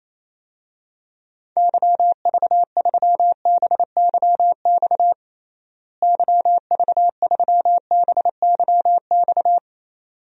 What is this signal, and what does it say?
Morse code, heard as a keyed tone, about 28 wpm, 700 Hz. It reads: YV3BYX YV3BYX
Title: DXCC-100 Call Signs - 6 Characters - Repeated 2x - 28wpm